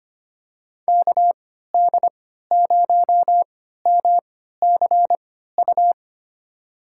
Morse code, heard as a keyed tone, about 25 wpm, spaced as for 14 wpm, 700 Hz. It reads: KD0MCU
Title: USA Call Signs - 6 Characters - 3x Character Spacing - 25wpm